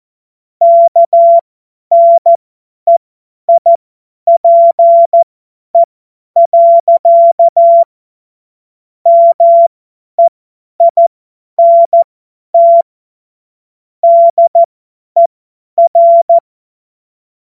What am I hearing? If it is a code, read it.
Morse code, heard as a keyed tone, about 14 wpm, spaced as for 10 wpm, 680 Hz. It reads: KNEIPE. MEINT DER
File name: Morsekurs / Witze / Witz 20. Juli 2021, 50 F70